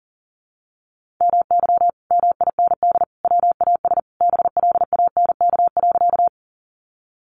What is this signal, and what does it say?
Morse code, heard as a keyed tone, about 40 wpm, 700 Hz. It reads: MY MIND WAS BLANK.
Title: Sentences from Top 1500 Words 40wpm